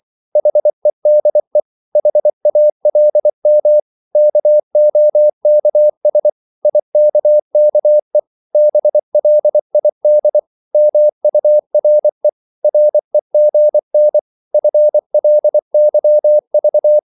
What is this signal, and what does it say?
Morse code, heard as a keyed tone, about 24 wpm, 600 Hz. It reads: HEDE HALM KOKS IKKE BLID MURE REGN FLYV
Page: Korte danske ord 24wpm